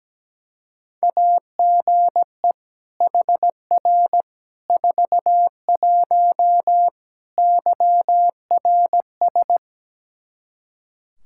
Morse code, heard as a keyed tone, about 17 wpm, 700 Hz. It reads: AGE HR 41 YRS
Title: Practice QSO - Exchange 3 - More Info - 17wpm